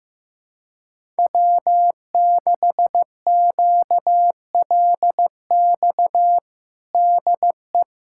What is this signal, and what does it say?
Morse code, heard as a keyed tone, about 15 wpm, 700 Hz. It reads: W6QLX DE
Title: USA Callsign DE Callsign - 5 Characters - 15wpm